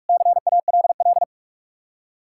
Morse code, heard as a keyed tone, about 58 wpm, spaced as for 51 wpm, 700 Hz. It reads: CALL
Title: Speed-Racing Common QSO Phrases - 45wpm